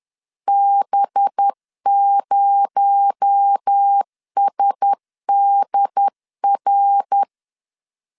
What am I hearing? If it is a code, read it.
Morse code, heard as a keyed tone, about 10 wpm, 800 Hz. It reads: B0SDR